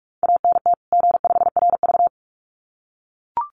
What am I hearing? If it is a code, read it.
Morse code, heard as a keyed tone, about 44 wpm, 700 Hz. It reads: ANT G5RV
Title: Common QSO Phrases 45wpm